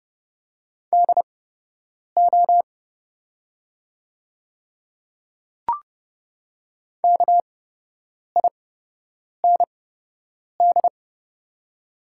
Morse code, heard as a keyed tone, about 30 wpm, spaced as for 8 wpm, 690 Hz. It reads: DO KIND
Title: 8x Character Spacing - Top 300 Words - 30wpm